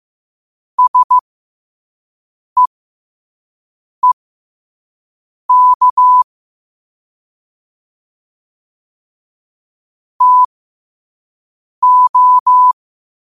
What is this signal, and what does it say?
Morse code, heard as a keyed tone, about 15 wpm, spaced as for 5 wpm, 1000 Hz. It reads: SEEK TO
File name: qotdfiles / 5wpm